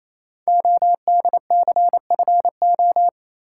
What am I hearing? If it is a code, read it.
Morse code, heard as a keyed tone, about 29 wpm, 700 Hz. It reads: ODCFO
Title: Code Groups - 5 Letters - 28wpm